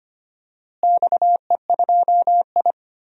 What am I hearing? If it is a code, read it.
Morse code, heard as a keyed tone, about 25 wpm, 700 Hz. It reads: XE2I
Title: DXCC-100 Call Signs - 4 Characters - Repeated 3x - 25wpm